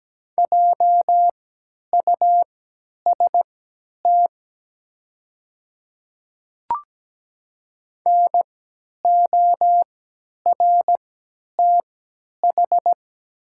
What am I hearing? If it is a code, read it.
Morse code, heard as a keyed tone, about 17 wpm, spaced as for 10 wpm, 700 Hz. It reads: JUST NORTH
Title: 3x Character Spacing - Top 500 Words - 17wpm